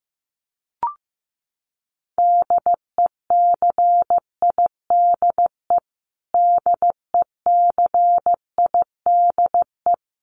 Morse code, heard as a keyed tone, about 15 wpm, 700 Hz. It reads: DECIDE DECIDE
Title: Top 500 Words Repeated 15wpm